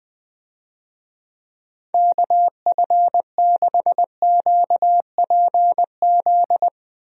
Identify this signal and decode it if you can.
Morse code, heard as a keyed tone, about 19 wpm, 700 Hz. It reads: KF6QPZ
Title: USA Call Signs - 6 Characters - Repeated 3x - 20wpm